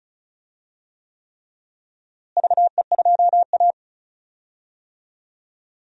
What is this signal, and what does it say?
Morse code, heard as a keyed tone, about 34 wpm, 700 Hz. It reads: VE2A
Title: DXCC-100 Call Signs - 4 Characters - Repeated 3x - 35wpm